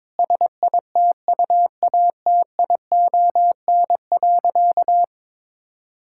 Morse code, heard as a keyed tone, about 22 wpm, 700 Hz. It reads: SITUATION.
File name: practice / 22wpm / Sentences with 8-Word Phrases - 22wpm